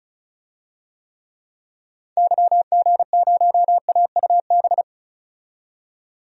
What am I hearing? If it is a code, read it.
Morse code, heard as a keyed tone, about 34 wpm, 700 Hz. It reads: YG0AUB
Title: DXCC-100 Call Signs - 6 Characters - 35wpm